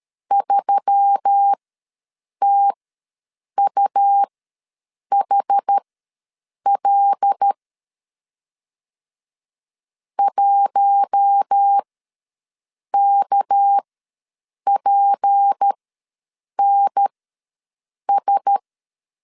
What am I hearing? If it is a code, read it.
Morse code, heard as a keyed tone, about 13 wpm, spaced as for 7 wpm, 800 Hz. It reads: 3TUHL 1KPNS
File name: 7 GF9Z ws 12wpm